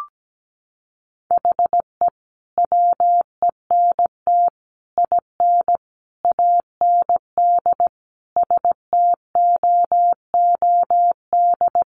Morse code, heard as a keyed tone, about 17 wpm, 700 Hz. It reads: HE WENT IN AND STOOD
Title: Sentences from Top 1300 Words 17wpm